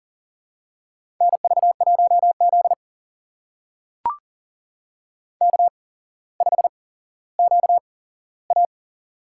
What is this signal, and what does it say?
Morse code, heard as a keyed tone, about 40 wpm, spaced as for 17 wpm, 700 Hz. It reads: NV1Z K5QA
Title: USA Call Signs - 8x Character Spacing - 40wpm